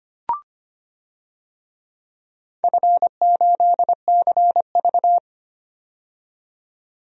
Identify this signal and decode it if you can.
Morse code, heard as a keyed tone, about 25 wpm, 700 Hz. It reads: F8CV
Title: DXCC-100 Call Signs - 4 Characters - 25wpm